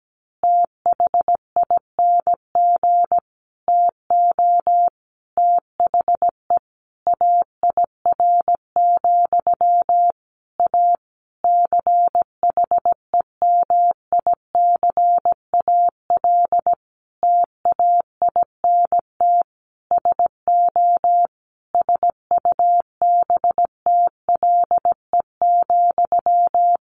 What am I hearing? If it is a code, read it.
Morse code, heard as a keyed tone, about 17 wpm, 700 Hz. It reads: THING TO THE AIR, A CHEMICAL TAINT SO SUBTLE,
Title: A Sound Of Thunder 17wpm